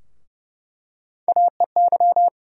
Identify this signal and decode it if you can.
Morse code, heard as a keyed tone, about 30 wpm, 700 Hz. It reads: AEY
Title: USA Call Sign Postfixes - 3 Letters 30wpm